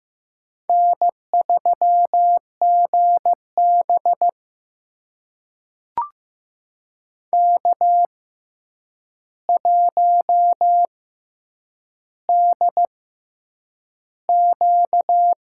Morse code, heard as a keyed tone, about 15 wpm, spaced as for 8 wpm, 700 Hz. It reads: N3GB K1DQ
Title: USA Call Signs - 4 Characters - 6x Character Spacing - 15wpm